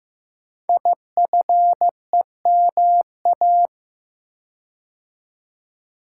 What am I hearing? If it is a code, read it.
Morse code, heard as a keyed tone, about 15 wpm, 700 Hz. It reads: IFEMA
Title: Code Groups - 5 Letter:Numbers - 15wpm